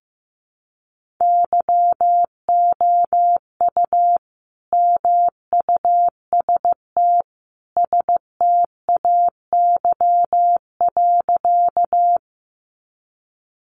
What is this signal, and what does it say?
Morse code, heard as a keyed tone, about 15 wpm, 700 Hz. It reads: YOU MUST STAY.
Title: Sentences from Top 500 Words 15wpm